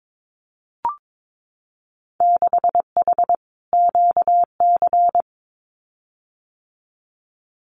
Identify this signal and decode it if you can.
Morse code, heard as a keyed tone, about 22 wpm, 700 Hz. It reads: 6H QC
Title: Winter Field Day Exchanges Encore 22wpm